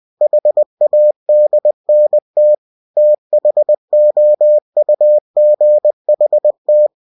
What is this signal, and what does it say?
Morse code, heard as a keyed tone, about 20 wpm, 600 Hz. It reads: HADNT THOUGHT
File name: cw-books / winnie-the-pooh / winnie-20wpm-page-0035